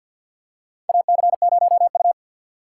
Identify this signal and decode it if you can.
Morse code, heard as a keyed tone, about 52 wpm, 690 Hz. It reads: AC0U
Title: USA Call Signs - 2x Character Spacing - 50wpm